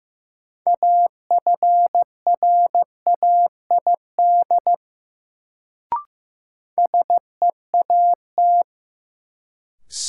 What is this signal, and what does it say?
Morse code, heard as a keyed tone, about 15 wpm, 700 Hz. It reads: AFRAID SEAT
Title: Sentences from Top 1500 Words - New Word Drill - 15wpm